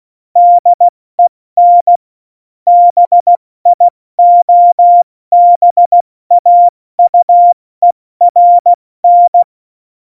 Morse code, heard as a keyed tone, about 16 wpm, 700 Hz. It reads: DEN BIOBAUERN
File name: Morsekurs / Witze / Witz 14 Juli 2020, 70 F80